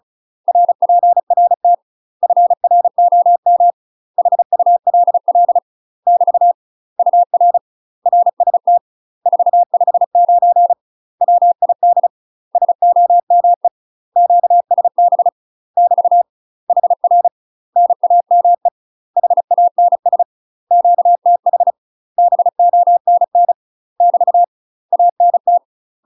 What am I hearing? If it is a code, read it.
Morse code, heard as a keyed tone, about 34 wpm, 700 Hz. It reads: RPRT FROM HULL <BT> UR RST 459 WID SOME QSB <BT> HR NAME HANS QTH BONN <BT> ANT
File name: UK MASTER QSOs TRAINING / 35WPM / UK MASTER QSOs 0001